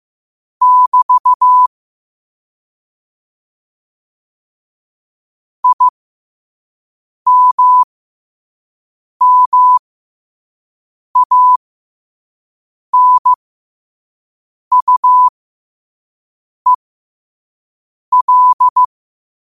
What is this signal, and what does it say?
Morse code, heard as a keyed tone, about 15 wpm, spaced as for 5 wpm, 1000 Hz. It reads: <BT> IMMANUEL
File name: qotdfiles / 5wpm